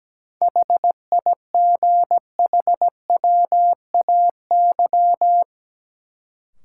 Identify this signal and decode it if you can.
Morse code, heard as a keyed tone, about 17 wpm, 700 Hz. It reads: HIGHWAY
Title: Sentences from Top 1400 Words - New Word Drill - 17wpm